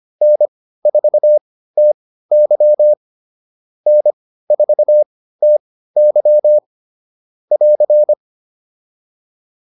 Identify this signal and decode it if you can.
Morse code, heard as a keyed tone, about 25 wpm, spaced as for 15 wpm, 600 Hz. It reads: N4TY N4TY <AR>